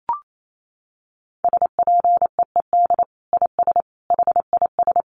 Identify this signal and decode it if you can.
Morse code, heard as a keyed tone, about 28 wpm, 700 Hz. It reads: SPEED IS HIS
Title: Sentences from Top 1300 Words 28wpm